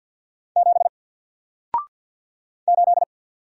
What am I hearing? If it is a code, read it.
Morse code, heard as a keyed tone, about 48 wpm, 700 Hz. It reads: 7 8
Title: Single Character (Letter:Number:Punctuation:Prosign) Alt 50wpm